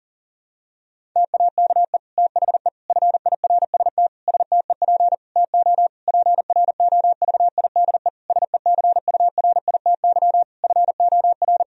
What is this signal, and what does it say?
Morse code, heard as a keyed tone, about 37 wpm, 700 Hz. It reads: TAKE THE FIRST STEP TO PROVIDE SECURITY FOR
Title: Sentences from Top 1000 Words - Review Part 4 - 40wpm